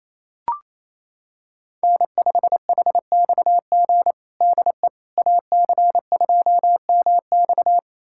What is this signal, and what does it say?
Morse code, heard as a keyed tone, about 28 wpm, 700 Hz. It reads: N5HXG DE AC2MX
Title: USA Callsign DE Callsign - 5 Characters - 28wpm